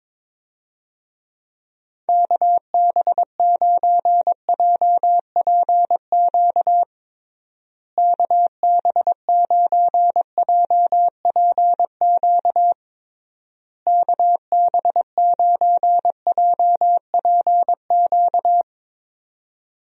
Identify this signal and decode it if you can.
Morse code, heard as a keyed tone, about 21 wpm, 700 Hz. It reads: KB9JPQ KB9JPQ KB9JPQ
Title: USA Call Signs - 6 Characters - Repeated 3x - 22wpm